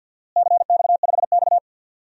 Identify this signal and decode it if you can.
Morse code, heard as a keyed tone, about 51 wpm, 700 Hz. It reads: CX5X
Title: DXCC-100 Call Signs - 4 Characters - Repeated 2x - 50wpm